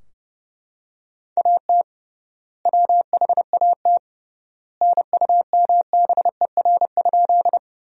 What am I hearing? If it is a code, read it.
Morse code, heard as a keyed tone, about 32 wpm, 700 Hz. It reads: AT WHAT NUMBER?
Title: 3x Word-Spacing - Sentences from Top 100 Words - 30wpm